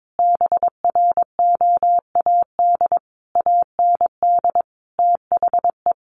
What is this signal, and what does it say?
Morse code, heard as a keyed tone, about 22 wpm, 700 Hz. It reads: BROAD AND THE
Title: A Sound Of Thunder 22wpm